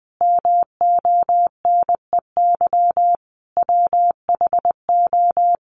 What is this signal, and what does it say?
Morse code, heard as a keyed tone, about 20 wpm, 700 Hz. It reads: MONEY WHO
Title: Sentences from Top 1200 Words 20wpm